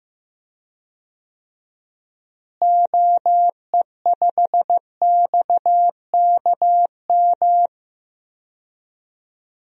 Morse code, heard as a keyed tone, about 15 wpm, 700 Hz. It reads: OE5XKM
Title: DXCC-100 Call Signs - 6 Characters - Repeated 3x - 15wpm